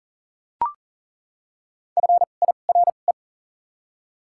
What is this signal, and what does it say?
Morse code, heard as a keyed tone, about 40 wpm, spaced as for 29 wpm, 700 Hz. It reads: FIRE
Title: 2x Character Spacing - Top 500 Words - 40wpm